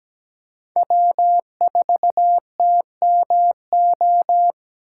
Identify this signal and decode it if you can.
Morse code, heard as a keyed tone, about 17 wpm, 700 Hz. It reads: W4TMO
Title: K1USN SST QSO Practice - 17wpm